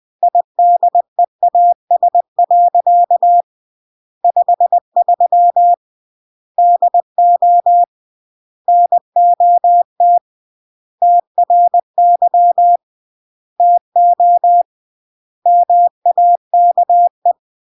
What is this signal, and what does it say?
Morse code, heard as a keyed tone, about 20 wpm, 700 Hz. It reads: IDEAS. 53 DO NOT TRY TO MAKE
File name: CW / jodorowsky / 82 commandments 20-20